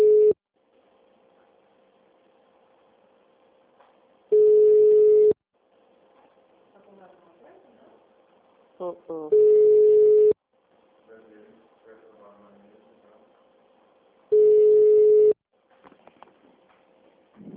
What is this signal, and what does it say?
Ringback tone (Europe, 425 Hz), with speech.
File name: out- -2007- - - .2873